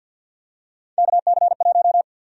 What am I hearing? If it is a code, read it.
Morse code, heard as a keyed tone, about 53 wpm, 700 Hz. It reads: KC1
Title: USA Call Sign Prefixes - 50wpm